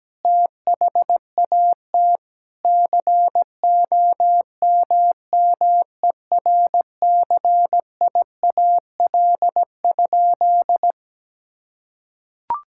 Morse code, heard as a keyed tone, about 17 wpm, 700 Hz. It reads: THAT COMMERCIAL?
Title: Sentences from Top 1000 Words - Review Part 5 - 17wpm